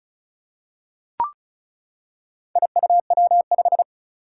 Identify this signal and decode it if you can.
Morse code, heard as a keyed tone, about 35 wpm, 700 Hz. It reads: IUW5
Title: Four Challenging Letter-Number Combinations 35wpm